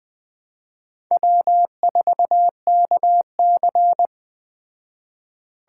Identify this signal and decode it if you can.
Morse code, heard as a keyed tone, about 20 wpm, 700 Hz. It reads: W4KC